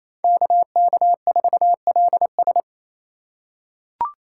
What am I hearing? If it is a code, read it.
Morse code, heard as a keyed tone, about 28 wpm, 700 Hz. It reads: KK4LS
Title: USA Call Signs - 5 Characters - 2x Character Spacing - 28wpm